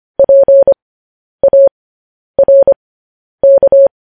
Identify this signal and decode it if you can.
Morse code, heard as a keyed tone, about 25 wpm, spaced as for 10 wpm, 550 Hz. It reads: PARK
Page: news headlines in morse code at 10 wpm